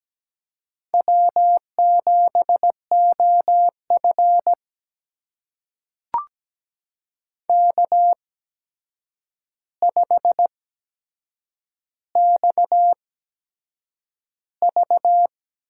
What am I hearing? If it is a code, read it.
Morse code, heard as a keyed tone, about 17 wpm, spaced as for 7 wpm, 700 Hz. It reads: W7OF K5XV
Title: USA Call Signs - 4 Characters - 8x Character Spacing - 17wpm